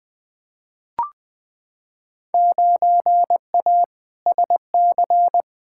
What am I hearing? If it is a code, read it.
Morse code, heard as a keyed tone, about 21 wpm, 700 Hz. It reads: 9A SC
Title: Field Day Exchanges Encore - 20wpm